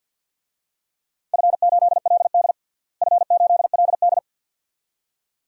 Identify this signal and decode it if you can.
Morse code, heard as a keyed tone, about 49 wpm, 700 Hz. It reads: F8LD F8LD
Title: DXCC-100 Call Signs - 4 Characters - Repeated 2x - 50wpm